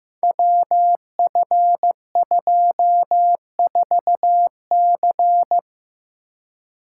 Morse code, heard as a keyed tone, about 15 wpm, 700 Hz. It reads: WF24C
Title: Code Groups - 5 Letter:Numbers - 15wpm